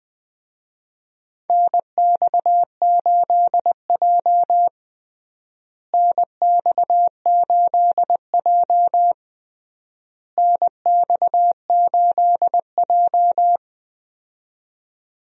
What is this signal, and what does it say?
Morse code, heard as a keyed tone, about 20 wpm, 700 Hz. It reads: NX8J NX8J NX8J
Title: USA Call Signs - 4 Characters - Repeated 3x - 20wpm